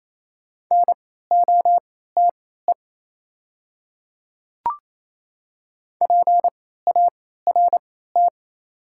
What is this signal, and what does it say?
Morse code, heard as a keyed tone, about 27 wpm, spaced as for 16 wpm, 700 Hz. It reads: NOTE PART
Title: 3x Character Spacing - Top 500 Words - 28wpm